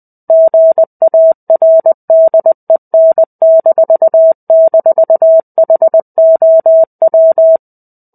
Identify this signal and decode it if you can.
Morse code, heard as a keyed tone, about 20 wpm, 650 Hz. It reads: GARDEN--HOW
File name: 05-Chapter-05 20 WPM